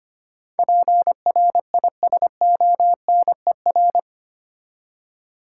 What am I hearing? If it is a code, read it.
Morse code, heard as a keyed tone, about 25 wpm, 700 Hz. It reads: PRISONER
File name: Eight Letter Words - 25wpm